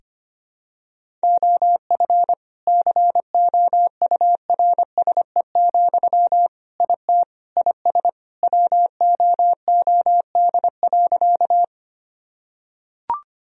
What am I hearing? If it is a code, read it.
Morse code, heard as a keyed tone, about 25 wpm, 700 Hz. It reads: OF COURSE, IT IS WOOD.
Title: Sentences from Top 1000 Words - Review Part 2 - 25wpm